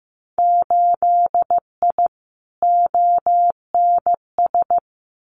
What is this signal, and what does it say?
Morse code, heard as a keyed tone, about 15 wpm, 700 Hz. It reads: 8I ONS
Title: Winter Field Day Exchanges Encore 15wpm